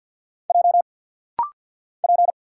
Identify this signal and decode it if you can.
Morse code, heard as a keyed tone, about 50 wpm, 700 Hz. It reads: J P
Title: Single Letters - Rapid-Fire 50wpm